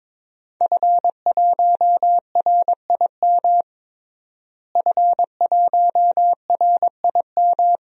Morse code, heard as a keyed tone, about 22 wpm, 700 Hz. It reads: F1RIM F1RIM
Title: DXCC-100 Call Signs - 5 Characters - Repeated 2x - 22wpm